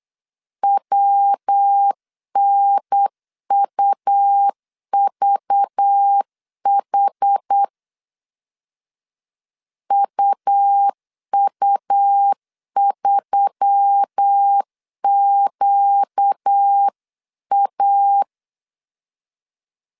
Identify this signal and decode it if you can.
Morse code, heard as a keyed tone, about 8 wpm, 800 Hz. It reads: WNUVH UU3QA
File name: 8wpm